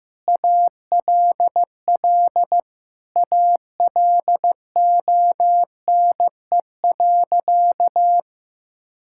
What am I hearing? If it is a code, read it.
Morse code, heard as a keyed tone, about 15 wpm, 700 Hz. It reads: ALL ALONE.
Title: Sentences from Top 1000 Words - Review Part 5 - 15wpm